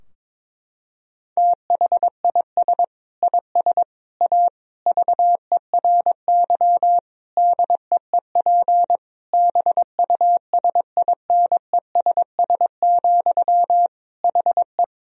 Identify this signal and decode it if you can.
Morse code, heard as a keyed tone, about 22 wpm, 700 Hz. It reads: THIS IS A VERY DEEP BUSINESS, HE
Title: Sentences from Top 1000 Words - Review Part 3 - 22wpm